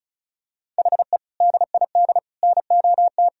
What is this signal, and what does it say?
Morse code, heard as a keyed tone, about 36 wpm, 700 Hz. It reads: HE DID NOT
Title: Sentences from Top 1000 Words - Review Part 1 - 35wpm